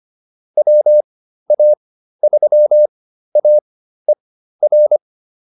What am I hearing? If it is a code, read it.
Morse code, heard as a keyed tone, about 25 wpm, spaced as for 13 wpm, 600 Hz. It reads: WA3AER